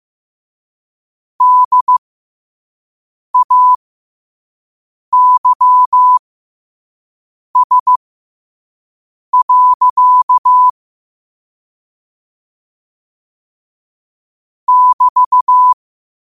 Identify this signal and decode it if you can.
Morse code, heard as a keyed tone, about 15 wpm, spaced as for 5 wpm, 1000 Hz. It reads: DAYS. <BT>
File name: qotdfiles / 5wpm